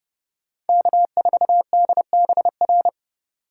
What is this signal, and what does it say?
Morse code, heard as a keyed tone, about 28 wpm, 700 Hz. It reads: K4DBR